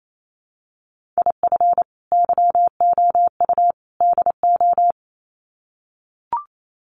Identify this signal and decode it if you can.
Morse code, heard as a keyed tone, about 28 wpm, 700 Hz. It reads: IF YOU DO
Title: Sets of 3 Words 28wpm